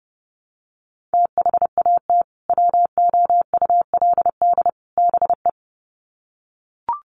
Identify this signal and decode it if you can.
Morse code, heard as a keyed tone, about 30 wpm, 700 Hz. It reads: THAT WOULD BE
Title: Sets of 3 Words 30wpm